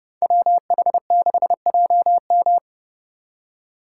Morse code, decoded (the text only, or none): WH6JM